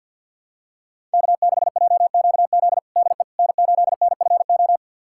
Morse code, heard as a keyed tone, about 51 wpm, 700 Hz. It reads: K6JQZ DE N8NFO
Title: USA Callsign DE Callsign - 5 Characters - 50wpm